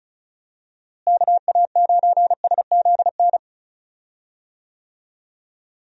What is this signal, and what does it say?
Morse code, heard as a keyed tone, about 34 wpm, 700 Hz. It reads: KA9SZN
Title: USA Call Signs - 6 Characters - Repeated 2x - 35wpm